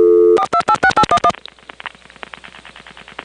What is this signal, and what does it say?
DTMF, heard as telephone touch tones, tones of about 70 ms. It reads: 4286811
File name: phonetrips / SmyrnaGA5HQ